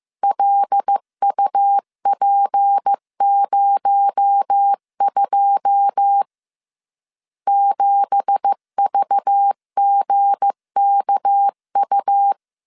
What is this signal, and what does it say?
Morse code, heard as a keyed tone, about 15 wpm, 800 Hz. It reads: LUP02 7VGKU
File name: CW / DET / 1 9 1500C S A 14wpm 3222